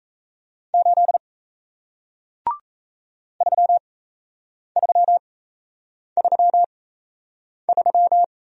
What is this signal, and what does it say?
Morse code, heard as a keyed tone, about 38 wpm, spaced as for 8 wpm, 700 Hz. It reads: 8 3333